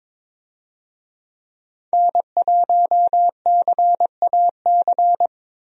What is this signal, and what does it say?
Morse code, heard as a keyed tone, about 21 wpm, 700 Hz. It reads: N1CAC
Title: USA Call Signs - 5 Characters - Repeated 2x - 22wpm